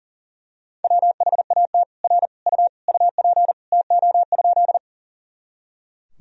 Morse code, heard as a keyed tone, about 40 wpm, 700 Hz. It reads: WHAT R U UP TO?
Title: Sentences from Top 100 Words and Abbreviations 40wpm